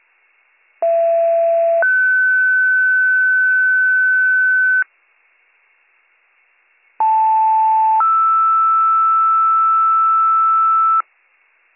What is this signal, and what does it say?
Two-tone page, heard as a one-second tone, then a three-second tone, about 670 then 1500 Hz, about 880 then 1320 Hz. It is heard from 0.8 to 11.0 s.